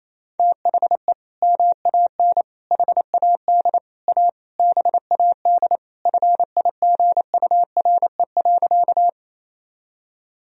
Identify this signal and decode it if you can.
Morse code, heard as a keyed tone, about 28 wpm, 700 Hz. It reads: THE MAN HAD A BAD FIGURE.
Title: Sentences from Top 1000 Words - Review Part 2 - 28wpm